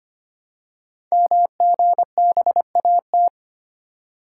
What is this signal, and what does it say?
Morse code, heard as a keyed tone, about 25 wpm, 700 Hz. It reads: MGBAT
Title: Code Groups - 5 Letter:Numbers - 25wpm